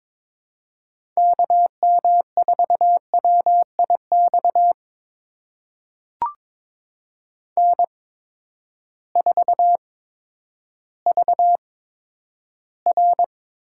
Morse code, heard as a keyed tone, about 22 wpm, spaced as for 11 wpm, 700 Hz. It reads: KM4WIX N4VR
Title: USA Call Signs - 8x Character Spacing - 22wpm